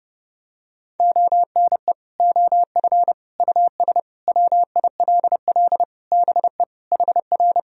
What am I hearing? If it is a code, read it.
Morse code, heard as a keyed tone, about 30 wpm, 700 Hz. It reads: ONE OF US WILL BE HR
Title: Sentences from Top 300 Words and Abbreviations 30wpm